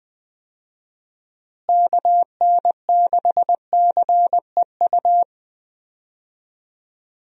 Morse code, heard as a keyed tone, about 20 wpm, 700 Hz. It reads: KN6CEU
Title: USA Call Signs - 6 Characters - Repeated 2x - 20wpm